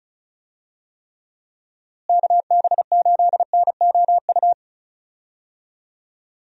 Morse code, heard as a keyed tone, about 36 wpm, 700 Hz. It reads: KB8NOU
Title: USA Call Signs - 6 Characters - Repeated 2x - 35wpm